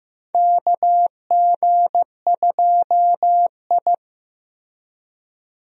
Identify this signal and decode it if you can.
Morse code, heard as a keyed tone, about 15 wpm, 700 Hz. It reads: KG2I
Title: USA Call Signs - 4 Characters - 15wpm